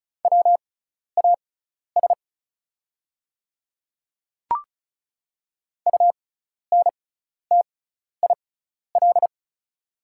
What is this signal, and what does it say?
Morse code, heard as a keyed tone, about 36 wpm, spaced as for 12 wpm, 700 Hz. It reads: WAS UNTIL
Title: 6x Character Spacing - Top 300 Words - 35wpm